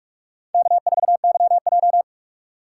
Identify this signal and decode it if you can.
Morse code, heard as a keyed tone, about 46 wpm, 700 Hz. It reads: K4YJ
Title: USA Call Signs - 4 Characters - 6x Character Spacing - 45wpm